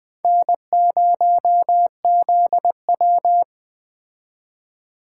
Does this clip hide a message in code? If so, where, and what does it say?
Yes: Morse code from 0.2 to 3.4 s, heard as a keyed tone, about 21 wpm, 700 Hz. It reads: N0ZW